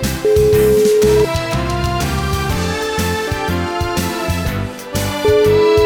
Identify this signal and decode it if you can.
Ringback tone (Europe, 425 Hz), with noise.